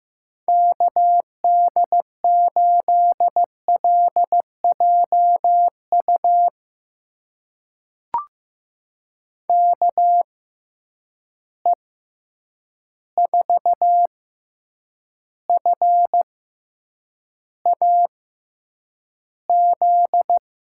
Morse code, heard as a keyed tone, about 15 wpm, spaced as for 8 wpm, 700 Hz. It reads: KD8LJU KE4FAZ
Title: USA Call Signs - 6 Characters - 6x Character Spacing - 15wpm